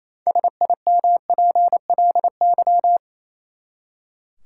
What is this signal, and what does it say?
Morse code, heard as a keyed tone, about 27 wpm, 700 Hz. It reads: SIMPLY
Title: Six Letter Words - 28wpm